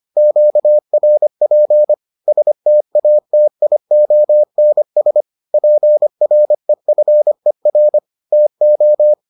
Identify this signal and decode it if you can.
Morse code, heard as a keyed tone, about 25 wpm, 600 Hz. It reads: QRP STATIONS PREFER TO